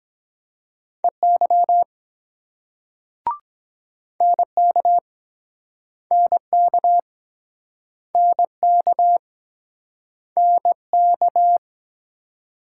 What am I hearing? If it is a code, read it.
Morse code, heard as a keyed tone, about 24 wpm, 700 Hz. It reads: EY NK NK NK NK